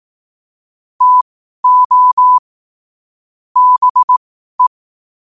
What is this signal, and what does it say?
Morse code, heard as a keyed tone, about 18 wpm, spaced as for 12 wpm, 1000 Hz. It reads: TO BE
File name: qotdfiles / 12wpm